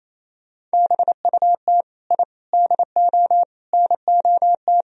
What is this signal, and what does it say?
Morse code, heard as a keyed tone, about 36 wpm, 700 Hz. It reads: BUT I DO NOT